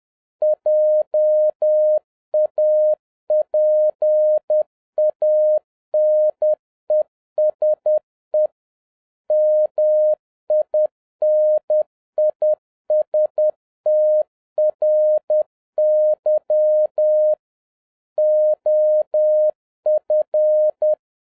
Morse code, heard as a keyed tone, about 10 wpm, 610 Hz. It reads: JAPANESE MINISTRY OF